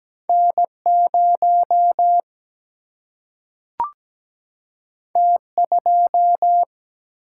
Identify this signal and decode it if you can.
Morse code, heard as a keyed tone, about 17 wpm, 700 Hz. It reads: N0 T2
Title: Single-Letter Single-Number 17wpm